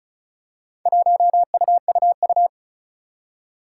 Morse code, heard as a keyed tone, about 36 wpm, 700 Hz. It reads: 1UUU
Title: Four Challenging Letter-Number Combinations 35wpm